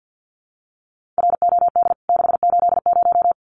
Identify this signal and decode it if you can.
Morse code, heard as a keyed tone, about 49 wpm, 700 Hz. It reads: ROD 680
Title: CWT Exchange 50wpm